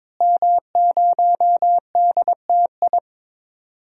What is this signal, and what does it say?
Morse code, heard as a keyed tone, about 22 wpm, 700 Hz. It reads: M0DTI
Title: DXCC-100 Call Signs - 5 Characters - Repeated 3x - 22wpm